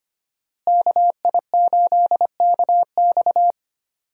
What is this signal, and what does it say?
Morse code, heard as a keyed tone, about 25 wpm, 700 Hz. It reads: KI8KX
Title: USA Call Signs - 5 Characters - Repeated 3x - 25wpm